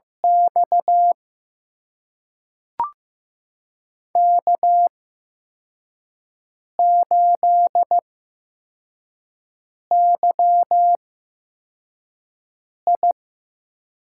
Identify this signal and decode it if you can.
Morse code, heard as a keyed tone, about 15 wpm, spaced as for 4 wpm, 700 Hz. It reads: X K8YI
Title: USA Call Signs - 8x Character Spacing - 15wpm